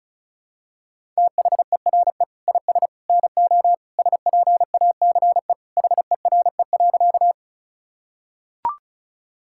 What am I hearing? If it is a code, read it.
Morse code, heard as a keyed tone, about 36 wpm, 700 Hz. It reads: THERE IS NO SPACE HERE.
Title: Sentences from Top 1000 Words - Review Part 5 - 35wpm